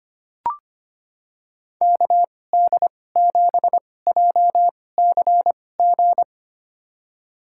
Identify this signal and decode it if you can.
Morse code, heard as a keyed tone, about 25 wpm, spaced as for 18 wpm, 700 Hz. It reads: KD7JCG
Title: USA Call Signs - 2x Character Spacing - 25wpm